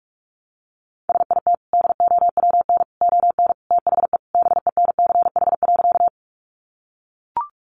Morse code, heard as a keyed tone, about 44 wpm, 700 Hz. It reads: SIT DOWN ON THE BENCH.
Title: Sentences from Top 1400 Words 45wpm